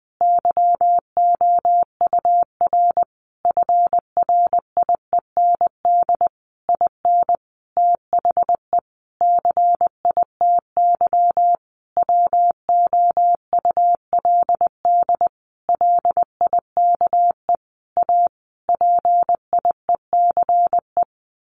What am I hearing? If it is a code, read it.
Morse code, heard as a keyed tone, about 20 wpm, 700 Hz. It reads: YOUR FRIEND IN THE CITY WOULD LIKE A PIECE
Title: Sentences from Top 1100 Words 20wpm